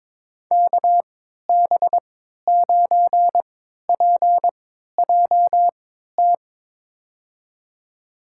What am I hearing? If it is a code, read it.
Morse code, heard as a keyed tone, about 21 wpm, spaced as for 12 wpm, 700 Hz. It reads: KB9PJT
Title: USA Call Signs - 3x Character Spacing - 22wpm